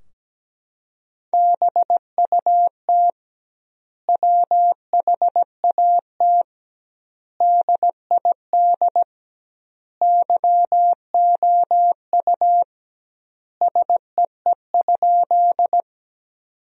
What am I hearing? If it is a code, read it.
Morse code, heard as a keyed tone, about 17 wpm, 700 Hz. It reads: BUT WHAT DID YOU SEE?
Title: 2x Word-Spacing - Sentences from Top 100 Words - 17wpm